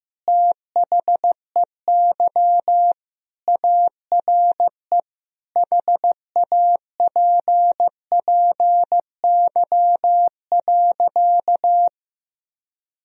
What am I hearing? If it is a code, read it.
Morse code, heard as a keyed tone, about 15 wpm, 700 Hz. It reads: THEY ARE HAPPY.
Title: Sentences from Top 1000 Words - Review Part 3 - 15wpm